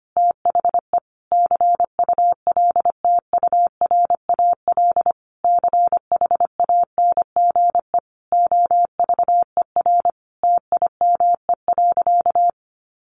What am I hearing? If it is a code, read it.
Morse code, heard as a keyed tone, about 25 wpm, 700 Hz. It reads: THE CULTURAL CHANGE OVER TIME.
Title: Sentences from Top 1500 Words 25wpm